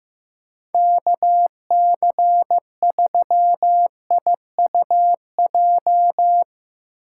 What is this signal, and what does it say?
Morse code, heard as a keyed tone, about 15 wpm, 700 Hz. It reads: KC3IUJ
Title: USA Call Signs - 6 Characters - 8x Character Spacing - 15wpm